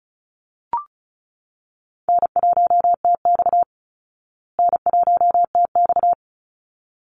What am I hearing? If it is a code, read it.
Morse code, heard as a keyed tone, about 34 wpm, 700 Hz. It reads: N1TX N1TX
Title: Call Signs Repeated 3x - 35wpm